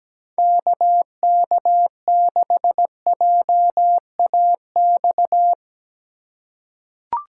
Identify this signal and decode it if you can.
Morse code, heard as a keyed tone, about 17 wpm, 700 Hz. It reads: KK6JAX
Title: USA Call Signs - 6x Character Spacing - 17wpm